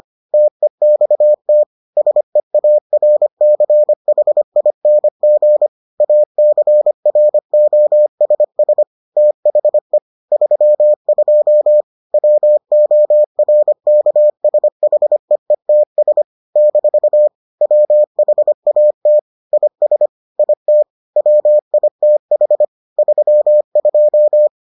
Morse code, heard as a keyed tone, about 25 wpm, 600 Hz. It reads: TEXT SEARCHING ACROSS THE 32 WORKSHEETS - WHAT IS IT WITH 32